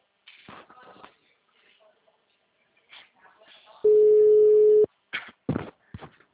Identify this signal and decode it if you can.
Ringback tone (Europe, 425 Hz).
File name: out- -2013- - - .49247